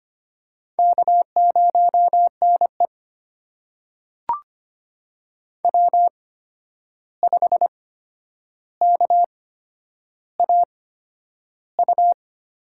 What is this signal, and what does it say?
Morse code, heard as a keyed tone, about 25 wpm, spaced as for 10 wpm, 700 Hz. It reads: K0NE W5KAU
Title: USA Call Signs - 8x Character Spacing - 25wpm